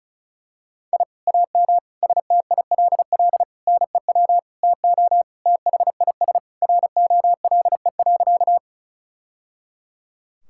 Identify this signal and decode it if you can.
Morse code, heard as a keyed tone, about 36 wpm, 700 Hz. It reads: I AM STILL NEW TO THIS ROLE.
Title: Sentences from Top 1000 Words - Review Part 6 - 35wpm